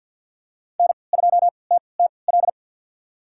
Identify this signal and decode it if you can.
Morse code, heard as a keyed tone, about 51 wpm, spaced as for 28 wpm, 690 Hz. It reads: N2TTL
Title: USA Call Signs - 5 Characters - 3x Character Spacing - 50wpm